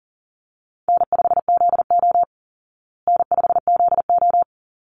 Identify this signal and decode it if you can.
Morse code, heard as a keyed tone, about 40 wpm, 700 Hz. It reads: N5ZO N5ZO
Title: Call Signs Repeated 3x - 40wpm